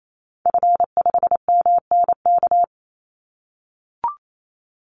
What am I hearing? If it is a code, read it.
Morse code, heard as a keyed tone, about 28 wpm, 700 Hz. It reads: F5MNK
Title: Call Signs - 28wpm